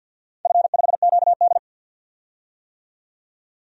Morse code, heard as a keyed tone, about 52 wpm, 700 Hz. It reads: F5QD
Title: DXCC-100 Call Signs - 4 Characters - 50wpm